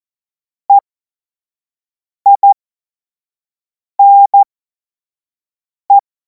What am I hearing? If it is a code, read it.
Morse code, heard as a keyed tone, about 14 wpm, spaced as for 5 wpm, 800 Hz. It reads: EINE
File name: Morsekurs / Witze / Witz 19. Mai 2026 25 F70